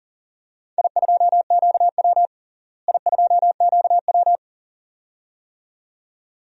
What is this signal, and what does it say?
Morse code, heard as a keyed tone, about 40 wpm, 700 Hz. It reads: I2QW I2QW
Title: DXCC-100 Call Signs - 4 Characters - Repeated 2x - 40wpm